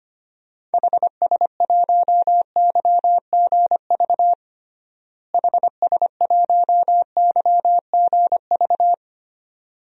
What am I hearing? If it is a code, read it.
Morse code, heard as a keyed tone, about 25 wpm, 700 Hz. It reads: HS1YGV HS1YGV
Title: DXCC-100 Call Signs - 6 Characters - Repeated 2x - 25wpm